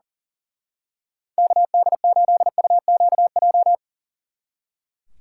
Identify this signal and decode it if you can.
Morse code, heard as a keyed tone, about 38 wpm, 700 Hz. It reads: KD8UQJ
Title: Speed Racing - USA 2x3 Call Signs 40wpm